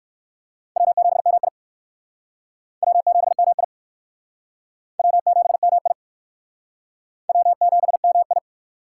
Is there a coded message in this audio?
Yes: Morse code from 0.8 to 8.4 s, heard as a keyed tone, about 56 wpm, 700 Hz. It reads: W7MI W7MI W7MI W7MI